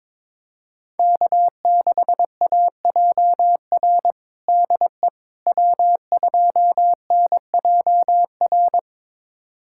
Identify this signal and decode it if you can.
Morse code, heard as a keyed tone, about 23 wpm, 700 Hz. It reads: K6AJR DE W2NJR
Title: USA Callsign DE Callsign - 5 Characters - 22wpm